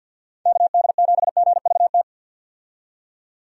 Morse code, heard as a keyed tone, about 51 wpm, 700 Hz. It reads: KD7GVT
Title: USA Call Signs - 6 Characters - 50wpm